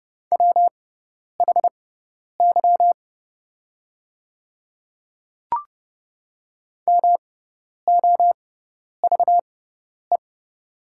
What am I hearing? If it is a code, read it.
Morse code, heard as a keyed tone, about 30 wpm, spaced as for 10 wpm, 690 Hz. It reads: WHY MOVE
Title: 6x Character Spacing - Top 200 Words - 30wpm